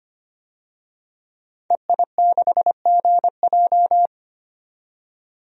Morse code, heard as a keyed tone, about 25 wpm, 690 Hz. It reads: EI6GJ
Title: DXCC-100 Call Signs - 5 Characters - 25wpm